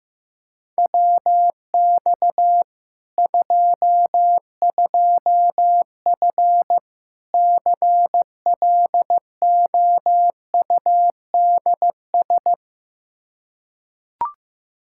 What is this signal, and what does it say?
Morse code, heard as a keyed tone, about 15 wpm, 700 Hz. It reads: WX 22F CLOUDS
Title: Practice QSO - Exchange 2 - Info - 15wpm